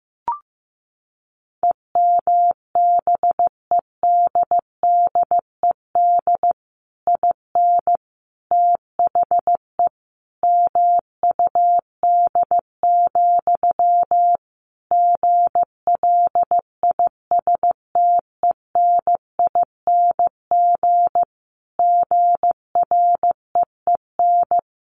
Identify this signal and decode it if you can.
Morse code, heard as a keyed tone, about 14 wpm, 700 Hz. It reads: EMBEDDED IN THE MUD, GLISTENING GREEN
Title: A Sound Of Thunder 15wpm